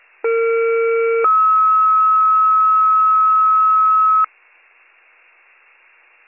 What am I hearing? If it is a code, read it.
Two-tone page, heard as a one-second tone, then a three-second tone, about 460 then 1290 Hz. It is heard from 0.2 to 4.3 s.